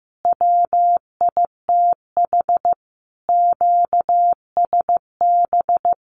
Morse code, heard as a keyed tone, about 15 wpm, 700 Hz. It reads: WITH QSB
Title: Common QSO Phrases Encore 15wpm